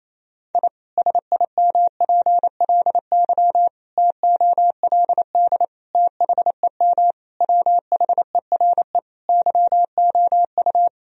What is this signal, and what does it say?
Morse code, heard as a keyed tone, about 28 wpm, 700 Hz. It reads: I SIMPLY TOLD THEM WHERE YOU
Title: Sentences from Top 1000 Words - Review Part 4 - 28wpm